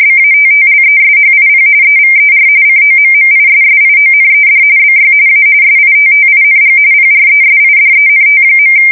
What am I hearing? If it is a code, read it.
RTTY, about 45.45 baud. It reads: BATTLESHIP EVER COMMISSIONED BY A SITTING PRESIDENT.
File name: fsk170 / USS-New-Jersey